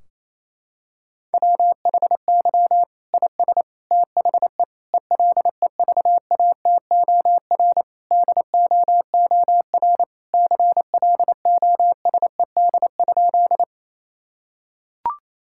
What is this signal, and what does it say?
Morse code, heard as a keyed tone, about 28 wpm, 700 Hz. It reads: WHY IS THE ELEVATOR DOOR CLOSED?
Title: Sentences from Top 1600 Words 28wpm